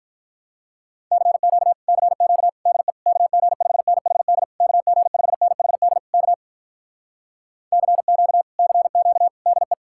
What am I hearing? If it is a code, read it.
Morse code, heard as a keyed tone, about 46 wpm, 700 Hz. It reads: CQ CQ DE KG5NHD KG5NHD K CQ CQ DE